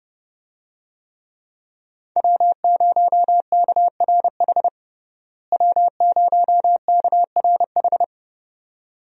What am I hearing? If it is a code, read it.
Morse code, heard as a keyed tone, about 32 wpm, 700 Hz. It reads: W0KRH W0KRH
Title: USA Call Signs - 5 Characters - Repeated 2x - 30wpm